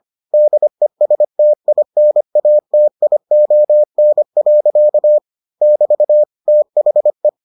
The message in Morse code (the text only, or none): DESTINATION. <BT> THE